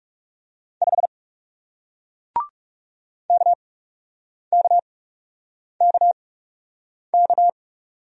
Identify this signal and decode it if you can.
Morse code, heard as a keyed tone, about 39 wpm, spaced as for 8 wpm, 700 Hz. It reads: 5 KKKK